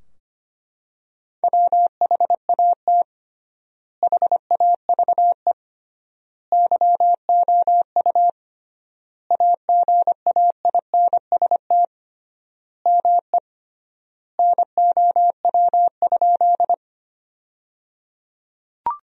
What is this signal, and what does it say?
Morse code, heard as a keyed tone, about 25 wpm, 700 Hz. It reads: WHAT HAVE YOU AGAINST ME NOW?
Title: 3x Word-Spacing - Sentences from Top 400 Words - 25wpm